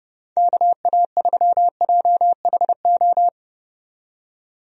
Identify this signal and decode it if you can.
Morse code, heard as a keyed tone, about 30 wpm, 700 Hz. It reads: KA3JHO
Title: USA Call Signs - 6 Characters - 30wpm